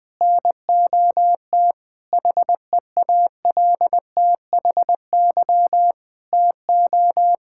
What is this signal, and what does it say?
Morse code, heard as a keyed tone, about 20 wpm, 700 Hz. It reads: NOT HEALTHY TO
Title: Sentences from Top 1000 Words - Review Part 6 - 20wpm